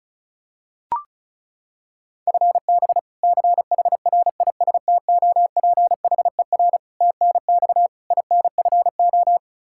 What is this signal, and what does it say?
Morse code, heard as a keyed tone, about 36 wpm, 700 Hz. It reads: FB CHRISTOPHER TNX INFO
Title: K1USN SST QSO Practice 35wpm